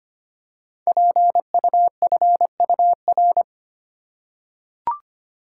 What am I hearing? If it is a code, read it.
Morse code, heard as a keyed tone, about 25 wpm, 700 Hz. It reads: PUFUR
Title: Code Groups - 5 Letters - 25wpm